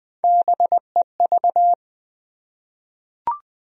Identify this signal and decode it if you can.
Morse code, heard as a keyed tone, about 21 wpm, 700 Hz. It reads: BEV
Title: USA Common Names - Max 3 Letters - 20wpm